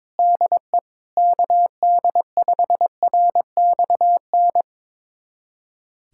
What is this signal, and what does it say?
Morse code, heard as a keyed tone, about 21 wpm, 700 Hz. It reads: DE KD5RXN
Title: USA Callsign DE Callsign 22wpm